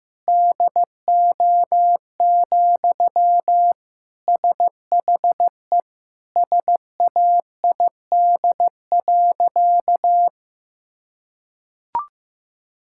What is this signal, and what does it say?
Morse code, heard as a keyed tone, about 15 wpm, 700 Hz. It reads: DO, SHE SAID.
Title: Sentences from Top 1000 Words - Review Part 5 - 15wpm